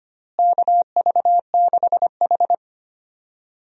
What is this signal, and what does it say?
Morse code, heard as a keyed tone, about 25 wpm, 700 Hz. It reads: KV6H
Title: USA Call Signs - 4 Characters - Repeated 3x - 25wpm